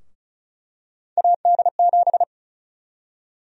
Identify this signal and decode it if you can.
Morse code, heard as a keyed tone, about 34 wpm, 700 Hz. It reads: AD7
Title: USA Call Sign Prefixes - Repeated 2x - 35wpm